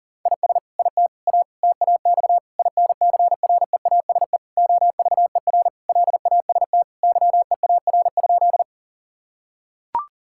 Morse code, heard as a keyed tone, about 40 wpm, 700 Hz. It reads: IS IT A TAX INCREASE OVER LAST YEAR?
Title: Sentences from Top 1000 Words - Review Part 5 - 40wpm